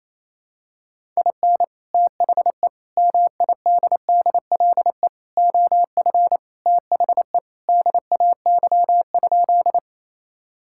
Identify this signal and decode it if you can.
Morse code, heard as a keyed tone, about 28 wpm, 700 Hz. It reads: IN THE MIDDLE OF THE DAY?
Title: Sentences from Top 1000 Words - Review Part 6 - 28wpm